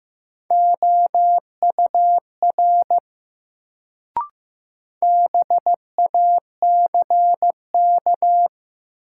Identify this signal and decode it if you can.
Morse code, heard as a keyed tone, about 15 wpm, 700 Hz. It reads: OUR BACK
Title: Sets of 2 Words - New Word Drill - 15wpm